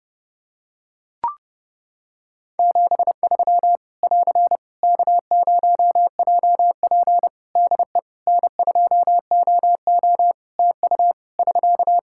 Morse code, heard as a keyed tone, about 30 wpm, 700 Hz. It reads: 73 <AR> K0JP DE N2OO TU <SK>
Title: Practice QSO - Exchange 4 - Ending - 30wpm